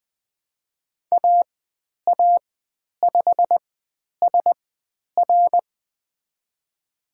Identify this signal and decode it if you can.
Morse code, heard as a keyed tone, about 20 wpm, spaced as for 10 wpm, 700 Hz. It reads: AA5SR
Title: 095 Lesson 16g - Letter c - Call Signs 20wpm